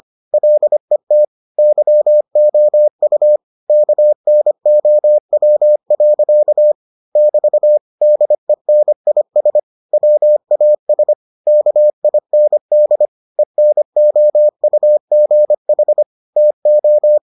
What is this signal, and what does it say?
Morse code, heard as a keyed tone, about 25 wpm, 600 Hz. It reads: LET YOU KNOW. <BT> DENIS WAS KIND ENOUGH TO